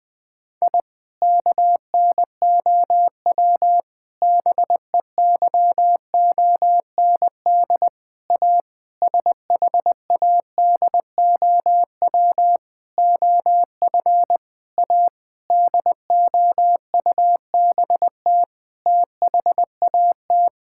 Morse code, heard as a keyed tone, about 20 wpm, 700 Hz. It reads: I KNOW BEYOND A SHADOW OF A DOUBT THAT
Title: Sentences with 7-Word Phrases - 20wpm